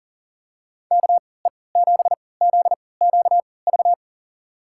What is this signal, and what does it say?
Morse code, heard as a keyed tone, about 40 wpm, spaced as for 22 wpm, 700 Hz. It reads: KE7ZQV
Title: USA Call Signs - 3x Character Spacing - 40wpm